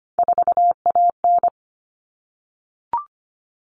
Morse code, heard as a keyed tone, about 25 wpm, 700 Hz. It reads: 4AN
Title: Common Cut Numbers - 3-Digit 25wpm